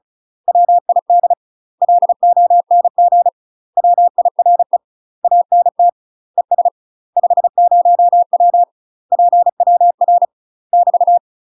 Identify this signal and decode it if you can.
Morse code, heard as a keyed tone, about 36 wpm, 700 Hz. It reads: WID LONG WIRE ANT ES 50W PWR <BT>
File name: UK MASTER QSOs TRAINING / 35WPM / UK MASTER QSOs 0036